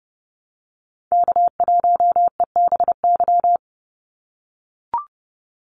Morse code, heard as a keyed tone, about 30 wpm, 700 Hz. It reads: K1EBY
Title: Call Signs - 30wpm